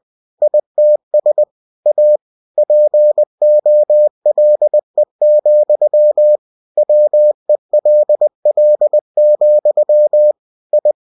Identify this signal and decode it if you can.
Morse code, heard as a keyed tone, about 20 wpm, 600 Hz. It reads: ITS A POLE, WELL, I
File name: cw-books / winnie-the-pooh / winnie-20wpm-page-0085